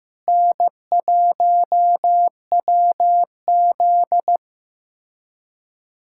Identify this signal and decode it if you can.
Morse code, heard as a keyed tone, about 15 wpm, 700 Hz. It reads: N1WZ
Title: USA Callsign DE Callsign - 4 Characters - 15wpm